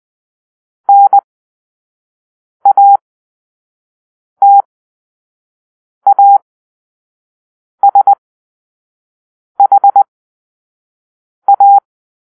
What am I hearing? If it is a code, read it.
Morse code, heard as a keyed tone, about 20 wpm, spaced as for 5 wpm, 800 Hz. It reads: NATASHA
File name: arrl / Grp-2 Drill-4